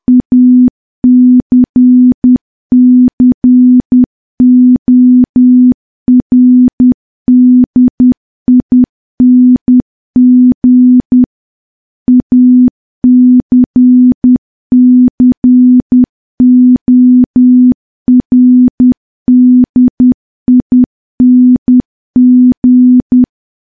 Morse code, heard as a keyed tone, about 10 wpm, 250 Hz. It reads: ACCORDING ACCORDING
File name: morse / morse-training-10WPM